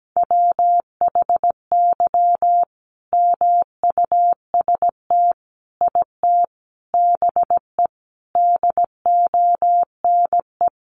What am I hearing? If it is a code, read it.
Morse code, heard as a keyed tone, about 17 wpm, 700 Hz. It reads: WHY MUST IT BE DONE
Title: Sentences from Top 600 Words 17wpm